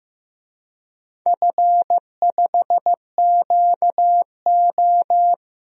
Morse code, heard as a keyed tone, about 15 wpm, 700 Hz. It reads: F5QO
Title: DXCC-100 Call Signs - 4 Characters - 15wpm